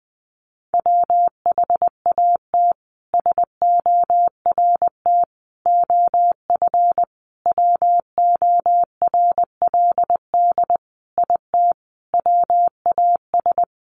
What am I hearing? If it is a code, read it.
Morse code, heard as a keyed tone, about 20 wpm, 700 Hz. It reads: WHAT SORT OF WORLD IT WAS
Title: A Sound Of Thunder 20wpm